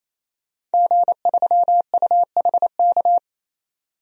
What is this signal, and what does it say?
Morse code, heard as a keyed tone, about 28 wpm, 700 Hz. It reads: G3UHK
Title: DXCC-100 Call Signs - 5 Characters - 28wpm